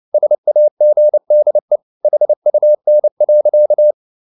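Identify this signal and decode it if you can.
Morse code, heard as a keyed tone, about 29 wpm, 600 Hz. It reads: SAGDE HUN.
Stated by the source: Fast (QRQ) Morse at 29 wpm, from CW med Gnister.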